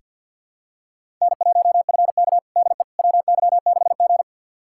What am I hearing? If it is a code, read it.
Morse code, heard as a keyed tone, about 49 wpm, 700 Hz. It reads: N1FK DE WY6G
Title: USA Callsign DE Callsign - 4 Characters - 50wpm